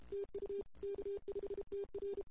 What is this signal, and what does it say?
Morse code, heard as a keyed tone, about 28 wpm, 410 Hz. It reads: TU K5TR
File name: CW / 2008-ss-cw-k5tr-op-n5rz -14:00